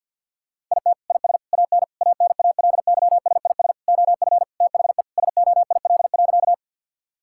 Morse code, heard as a keyed tone, about 46 wpm, 700 Hz. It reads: IT IS AN ANALYSIS OF THE SOIL.
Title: Sentences from Top 1000 Words - Review Part 5 - 50wpm